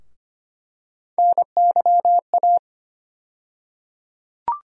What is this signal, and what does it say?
Morse code, heard as a keyed tone, about 25 wpm, 700 Hz. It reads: NYA
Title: USA Call Sign Postfixes - 3 Letters 25wpm